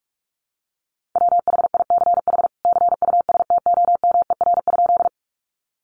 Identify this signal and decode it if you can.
Morse code, heard as a keyed tone, about 44 wpm, 700 Hz. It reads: WHICH CUSTOMER?
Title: Sentences from Top 1100 Words 45wpm